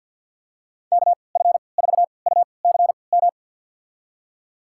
Morse code, heard as a keyed tone, about 48 wpm, spaced as for 28 wpm, 700 Hz. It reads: KF4UCM
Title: USA Call Signs - 6 Characters - 3x Character Spacing - 50wpm